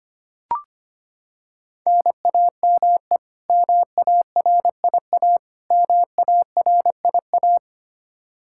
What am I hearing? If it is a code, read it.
Morse code, heard as a keyed tone, about 25 wpm, 700 Hz. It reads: NAME MARIA MARIA
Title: Practice QSO - Exchange 1 - Beginning - 25wpm